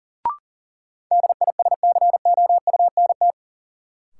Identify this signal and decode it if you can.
Morse code, heard as a keyed tone, about 40 wpm, 700 Hz. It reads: DISCOUNT
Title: Sentences from Top 1200 Words - New Word Drill - 40wpm